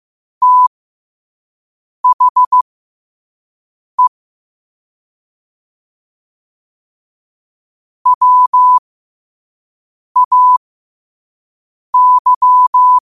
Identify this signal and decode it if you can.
Morse code, heard as a keyed tone, about 15 wpm, spaced as for 5 wpm, 1000 Hz. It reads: THE WAY
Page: Quotes for Tue, 12 Aug 2025 in Morse Code at 5 words per minute.